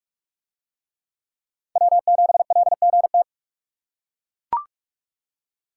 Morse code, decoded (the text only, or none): W7LGT